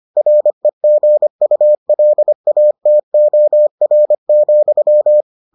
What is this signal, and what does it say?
Morse code, heard as a keyed tone, about 25 wpm, 600 Hz. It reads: REGULATOR,